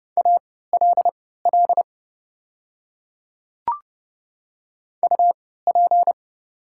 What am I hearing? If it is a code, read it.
Morse code, heard as a keyed tone, about 30 wpm, spaced as for 17 wpm, 700 Hz. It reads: ALL UP
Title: 3x Character Spacing - Top 100 Words - 30wpm